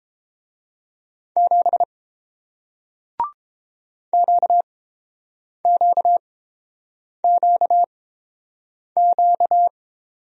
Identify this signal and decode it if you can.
Morse code, heard as a keyed tone, about 30 wpm, spaced as for 7 wpm, 700 Hz. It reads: 7 QQQQ